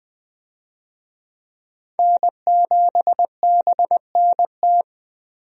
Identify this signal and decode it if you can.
Morse code, heard as a keyed tone, about 20 wpm, 700 Hz. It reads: N7BNT
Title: USA Call Signs - 5 Characters - Repeated 2x - 20wpm